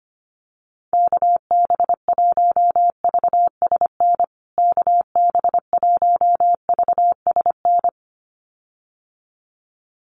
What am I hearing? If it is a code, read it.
Morse code, heard as a keyed tone, about 25 wpm, 700 Hz. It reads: KB1VSN KB1VSN
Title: USA Call Signs Repeated 2x 25wpm